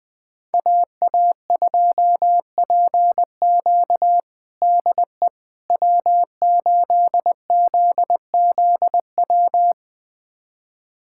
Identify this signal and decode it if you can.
Morse code, heard as a keyed tone, about 20 wpm, 700 Hz. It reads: AA2PQ DE W8ZZW
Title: USA Callsign DE Callsign 20wpm